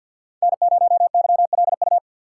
Morse code, heard as a keyed tone, about 52 wpm, 700 Hz. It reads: N0YLU
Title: USA Call Signs - 5 Characters - 2x Character Spacing - 50wpm